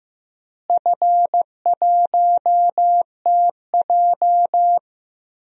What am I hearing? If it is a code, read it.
Morse code, heard as a keyed tone, about 15 wpm, 700 Hz. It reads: F1TJ
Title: DXCC-100 Call Signs - 4 Characters - Repeated 3x - 15wpm